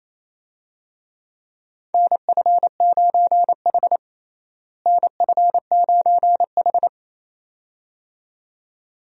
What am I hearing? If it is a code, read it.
Morse code, heard as a keyed tone, about 28 wpm, 700 Hz. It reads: NF9H NF9H
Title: USA Call Signs - 4 Characters - Repeated 2x - 28wpm